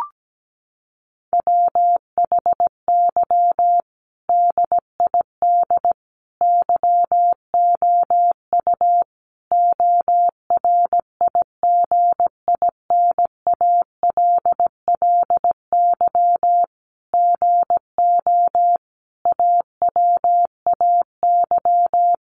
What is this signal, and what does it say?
Morse code, heard as a keyed tone, about 17 wpm, 700 Hz. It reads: WHY DID YOU ORIGINALLY GO AWAY
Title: Sentences from Top 1500 Words 17wpm